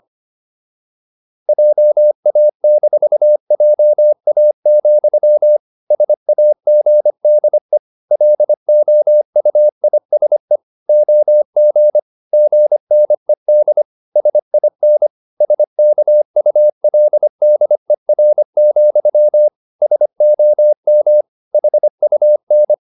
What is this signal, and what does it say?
Morse code, heard as a keyed tone, about 25 wpm, 600 Hz. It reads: JA-JA, SAGDE LOUISE OG GNED SIN SKULDER, SOM HUN
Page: Ferie 25 wpm